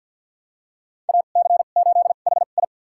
Morse code, heard as a keyed tone, about 46 wpm, spaced as for 35 wpm, 700 Hz. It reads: AC8HI
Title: USA Call Signs - 5 Characters - 2x Character Spacing - 50wpm